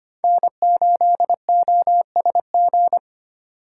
Morse code, heard as a keyed tone, about 29 wpm, 700 Hz. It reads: N8OSG
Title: Speed Racing - USA Call Signs 25wpm